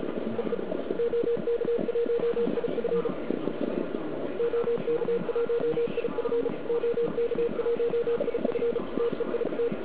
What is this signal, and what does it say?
Morse code, heard as a keyed tone, about 33 wpm, 470 Hz, with noise. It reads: EE OK1IF OK1IF OK1IF TEIT
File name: ARCHIV / KX3 / KV PA